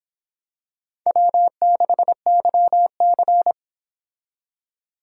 Morse code, heard as a keyed tone, about 25 wpm, 700 Hz. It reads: W6YC